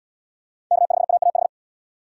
Morse code, heard as a keyed tone, about 73 wpm, 700 Hz. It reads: K5AAH